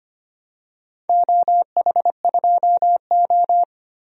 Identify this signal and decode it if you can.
Morse code, heard as a keyed tone, about 25 wpm, 700 Hz. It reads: OH2O
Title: DXCC-100 Call Signs - 4 Characters - 25wpm